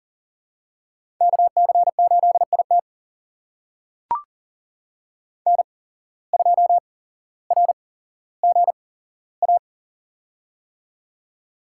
Morse code, heard as a keyed tone, about 40 wpm, spaced as for 17 wpm, 700 Hz. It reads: KC8IT N2RGA
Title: USA Call Signs - 5 Characters - 8x Character Spacing - 40wpm